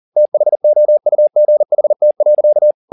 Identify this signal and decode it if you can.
Morse code, heard as a keyed tone, about 40 wpm, 600 Hz. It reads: THOUGHT.
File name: the old man and the sea 2 40wpm